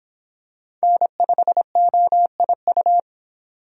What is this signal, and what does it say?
Morse code, heard as a keyed tone, about 25 wpm, 700 Hz. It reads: N5OIU